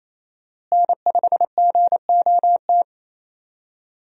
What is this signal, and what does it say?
Morse code, heard as a keyed tone, about 27 wpm, 700 Hz. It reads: N5GOT
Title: USA Call Signs - 5 Characters - Repeated 2x - 28wpm